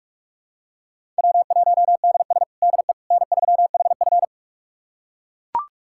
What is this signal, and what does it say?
Morse code, heard as a keyed tone, about 41 wpm, 700 Hz. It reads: W1DS DE N3HF
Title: USA Callsign DE Callsign - 4 Characters - 45wpm